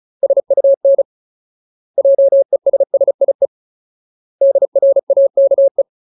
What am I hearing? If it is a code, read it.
Morse code, heard as a keyed tone, about 35 wpm, 550 Hz. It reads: SUN JESSIE DRAKE
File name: Ultimate Word List / 35WPM Ultimate